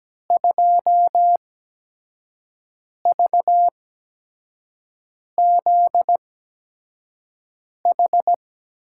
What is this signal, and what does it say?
Morse code, heard as a keyed tone, about 17 wpm, spaced as for 5 wpm, 700 Hz. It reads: 2VZH
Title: USA Call Signs - 5 Characters - 8x Character Spacing - 17wpm